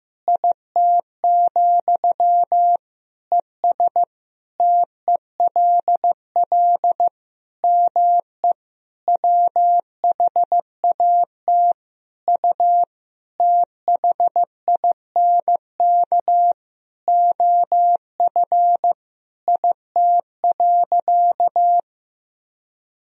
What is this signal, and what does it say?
Morse code, heard as a keyed tone, about 16 wpm, 700 Hz. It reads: IT, ES TELL ME WHAT U THINK OF IT.
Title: Sentences from Top 200 Words and Abbreviations 15wpm